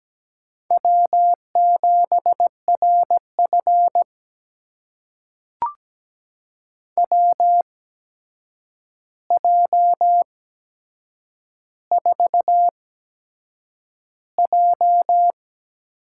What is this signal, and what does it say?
Morse code, heard as a keyed tone, about 17 wpm, spaced as for 7 wpm, 700 Hz. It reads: W7RF WJ4J
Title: USA Call Signs - 4 Characters - 8x Character Spacing - 17wpm